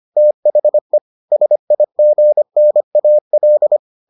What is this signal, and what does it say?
Morse code, heard as a keyed tone, about 25 wpm, 600 Hz. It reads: THE SIGNAL